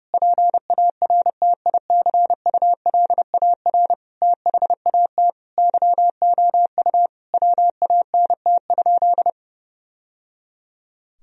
Morse code, heard as a keyed tone, about 30 wpm, 700 Hz. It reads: PARTICULAR THAT YOU WANT?
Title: Sentences from Top 1000 Words - Review Part 4 - 30wpm